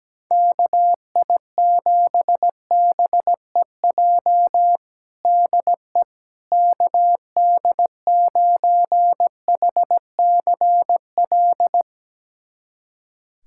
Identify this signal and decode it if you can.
Morse code, heard as a keyed tone, about 17 wpm, 700 Hz. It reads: KI7BEJ DE KD9HCL
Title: USA Callsign DE Callsign - 6 Characters - 17wpm